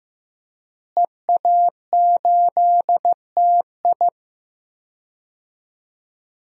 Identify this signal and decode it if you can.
Morse code, heard as a keyed tone, about 15 wpm, 700 Hz. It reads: EA8TI
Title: DXCC-100 Call Signs - 5 Characters - Repeated 2x - 15wpm